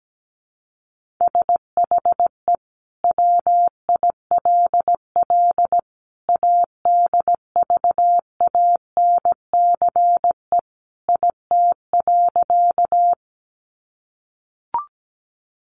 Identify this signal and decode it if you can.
Morse code, heard as a keyed tone, about 17 wpm, 700 Hz. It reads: SHE WILL ADVANCE IT.
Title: Sentences from Top 1400 Words 17wpm